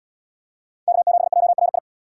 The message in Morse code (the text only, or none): K7JRI